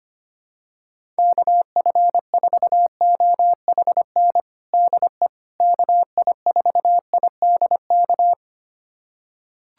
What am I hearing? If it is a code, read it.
Morse code, heard as a keyed tone, about 25 wpm, 700 Hz. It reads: KF4OHN DE KI4IDK
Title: USA Callsign DE Callsign - 6 Characters - 25wpm